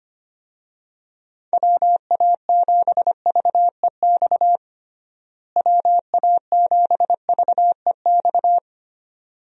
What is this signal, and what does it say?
Morse code, heard as a keyed tone, about 25 wpm, 700 Hz. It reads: WA7VEX WA7VEX
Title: USA Call Signs - 6 Characters - Repeated 2x - 25wpm